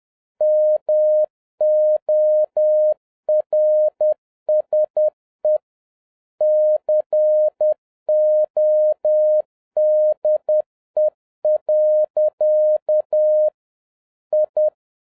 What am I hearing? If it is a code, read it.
Morse code, heard as a keyed tone, about 10 wpm, 610 Hz. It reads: MORSE CODE. I